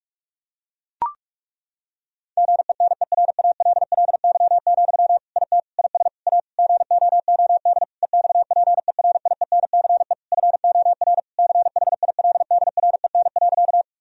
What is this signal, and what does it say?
Morse code, heard as a keyed tone, about 41 wpm, 700 Hz. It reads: GENERALLY, IT IS A GOOD EXPERIENCE FOR CHILDREN.
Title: Sentences from Top 1000 Words - Review Part 4 - 45wpm